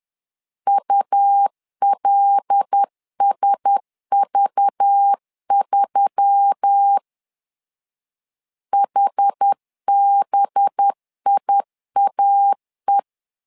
Morse code, heard as a keyed tone, about 10 wpm, 800 Hz. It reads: ULSV3 HBIAE
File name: LEZIONI / 5 LBVC / 5 LBV 10wpm 2248